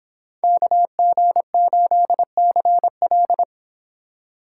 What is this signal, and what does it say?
Morse code, heard as a keyed tone, about 25 wpm, 700 Hz. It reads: KG8CL
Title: Speed Racing - USA 2x2 Call Signs 22wpm